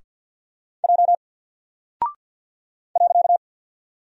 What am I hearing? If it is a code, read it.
Morse code, heard as a keyed tone, about 49 wpm, 700 Hz. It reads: J .
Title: Single Character (Letter:Number:Punctuation:Prosign) Alt 50wpm